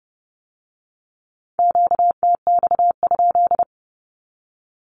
Morse code, heard as a keyed tone, about 30 wpm, 700 Hz. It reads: QTX?
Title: Q-Codes with Spoken Meaning - 30wpm